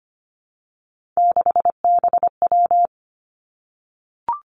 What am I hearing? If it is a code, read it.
Morse code, heard as a keyed tone, about 25 wpm, 700 Hz. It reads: 6BW
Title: Three Challenging Letter-Number Combinations 25wpm